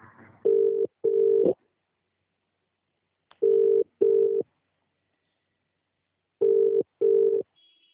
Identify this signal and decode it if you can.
Ringback tone (UK), 400 + 450 Hz.